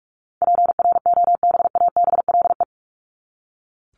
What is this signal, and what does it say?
Morse code, heard as a keyed tone, about 44 wpm, 700 Hz. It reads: PROBABLE
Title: LY Suffix - Top 100 Words with Base Words 45wpm